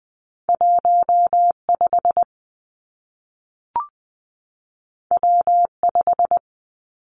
Morse code, heard as a keyed tone, about 19 wpm, 700 Hz. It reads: 1 5 W 5
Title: Two Challenging Letter-Number Combinations 20wpm